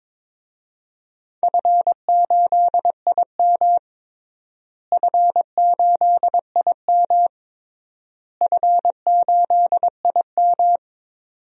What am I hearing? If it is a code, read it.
Morse code, heard as a keyed tone, about 22 wpm, 700 Hz. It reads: F8IM F8IM F8IM
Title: DXCC-100 Call Signs - 4 Characters - Repeated 3x - 22wpm